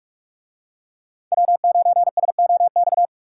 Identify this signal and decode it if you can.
Morse code, heard as a keyed tone, about 41 wpm, 700 Hz. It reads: W9SOX